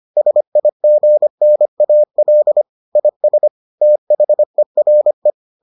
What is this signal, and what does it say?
Morse code, heard as a keyed tone, about 25 wpm, 600 Hz. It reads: SIGNAL IS THERE